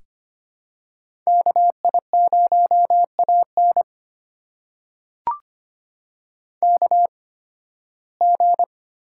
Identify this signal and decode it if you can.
Morse code, heard as a keyed tone, about 25 wpm, spaced as for 16 wpm, 700 Hz. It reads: KI0AN KG
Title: USA Call Signs - 5 Characters - 8x Character Spacing - 25wpm